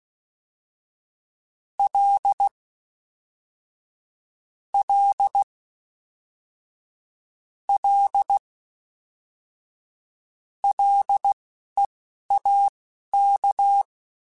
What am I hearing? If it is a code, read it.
Morse code, heard as a keyed tone, about 16 wpm, spaced as for 11 wpm, 800 Hz. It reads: L L L LEAK